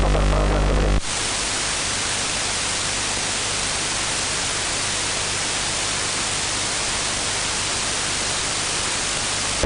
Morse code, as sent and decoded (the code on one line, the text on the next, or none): none